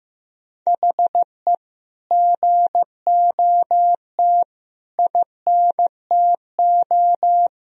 Morse code, heard as a keyed tone, about 15 wpm, 700 Hz. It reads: HE GOT INTO
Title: Sentences from Top 1000 Words - Review Part 1 - 15wpm